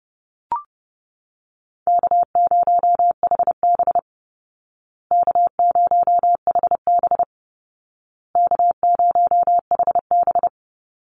Morse code, heard as a keyed tone, about 30 wpm, 700 Hz. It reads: K0HB K0HB K0HB
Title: Easy Call Signs Repeated 3x - 30wpm